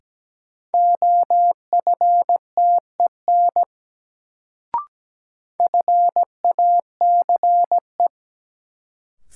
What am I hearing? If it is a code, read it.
Morse code, heard as a keyed tone, about 17 wpm, 700 Hz. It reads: OFTEN FACE
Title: Sentences from Top 300 Words - New Word Drill - 17wpm